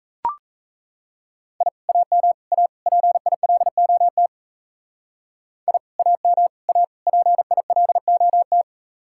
Morse code, heard as a keyed tone, about 34 wpm, 700 Hz. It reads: I AM A PILOT I AM A PILOT
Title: Speed-Racing Common QSO Phrases Encore - 28wpm